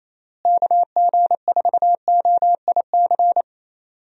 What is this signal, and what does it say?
Morse code, heard as a keyed tone, about 28 wpm, 700 Hz. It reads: KG4OIC
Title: USA Call Signs - 6 Characters - Repeated 2x - 28wpm